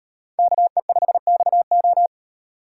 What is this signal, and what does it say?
Morse code, heard as a keyed tone, about 36 wpm, 700 Hz. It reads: KE5XO